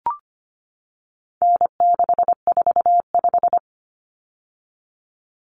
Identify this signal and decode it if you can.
Morse code, heard as a keyed tone, about 25 wpm, 700 Hz. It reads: N645
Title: Common Cut Numbers - 4-Digit 25wpm